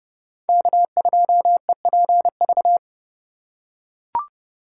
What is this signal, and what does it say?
Morse code, heard as a keyed tone, about 32 wpm, 700 Hz. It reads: K2EPV
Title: USA Call Signs - 5 Characters - 3x Character Spacing - 30wpm